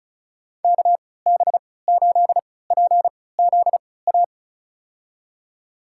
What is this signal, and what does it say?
Morse code, heard as a keyed tone, about 36 wpm, spaced as for 20 wpm, 700 Hz. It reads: KB8PZA
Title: USA Call Signs - 6 Characters - 3x Character Spacing - 35wpm